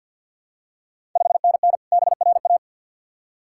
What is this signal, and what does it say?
Morse code, heard as a keyed tone, about 53 wpm, 700 Hz. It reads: 5NN BRA
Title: Ontario QSO Party Exchange 50wpm